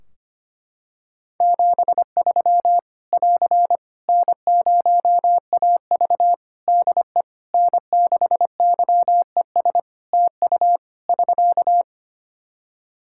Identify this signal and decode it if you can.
Morse code, heard as a keyed tone, about 25 wpm, 700 Hz. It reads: 73 <AR> N0AV DE N6YES TU <SK>
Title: Practice QSO - Exchange 4 - Ending - 25wpm